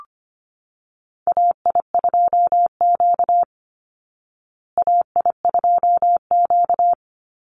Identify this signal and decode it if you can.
Morse code, heard as a keyed tone, about 25 wpm, 700 Hz. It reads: AI2Q AI2Q
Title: Call Signs Repeated 3x - 25wpm